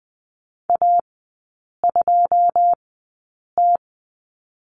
Morse code, heard as a keyed tone, about 20 wpm, spaced as for 8 wpm, 690 Hz. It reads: A2T
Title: 081 Lesson 14e - Number 2 - Three Characters 20wpm